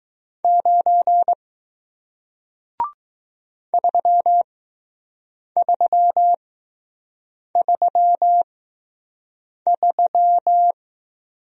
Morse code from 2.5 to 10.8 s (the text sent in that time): 3333